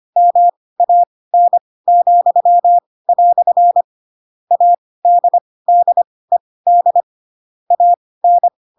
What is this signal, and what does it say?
Morse code, heard as a keyed tone, about 25 wpm, spaced as for 18 wpm, 700 Hz. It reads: MAN," ADDED AN
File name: Pinocchio ch3 12 25 ik1hge